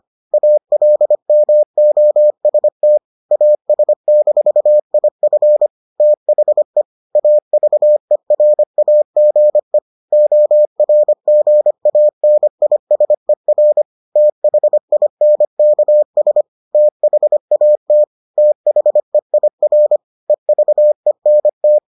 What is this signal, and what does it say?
Morse code, heard as a keyed tone, about 25 wpm, 600 Hz. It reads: ALMOST AS-IF THE AVERAGE ORGANISER THINKS THAT THEIR EVENT